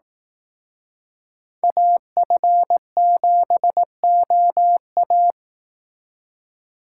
Morse code, heard as a keyed tone, about 18 wpm, 700 Hz. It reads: AF7OA
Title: Speed Racing - USA Call Signs 15wpm